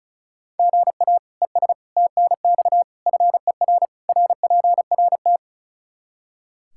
Morse code, heard as a keyed tone, about 36 wpm, 700 Hz. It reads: GA ES TNX FER RPRT
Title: Practice QSO - Exchange 1 - Beginning - 35wpm